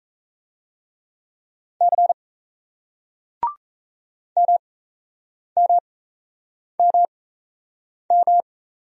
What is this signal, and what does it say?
Morse code, heard as a keyed tone, about 36 wpm, spaced as for 8 wpm, 700 Hz. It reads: C MMMM